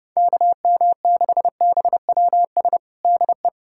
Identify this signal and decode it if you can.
Morse code, heard as a keyed tone, about 30 wpm, 700 Hz. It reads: KM6BWS DE
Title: USA Callsign DE Callsign - 6 Characters - 30wpm